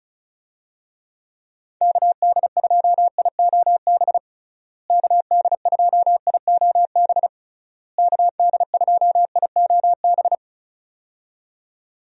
Morse code, heard as a keyed tone, about 34 wpm, 700 Hz. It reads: KD2IOB KD2IOB KD2IOB
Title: USA Call Signs - 6 Characters - Repeated 3x - 35wpm